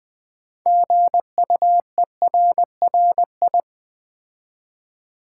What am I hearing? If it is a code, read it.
Morse code, heard as a keyed tone, about 20 wpm, 700 Hz. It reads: GUERRI
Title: Build A Word - 3 Letters At A Time - 9-Letter Words - 20wpm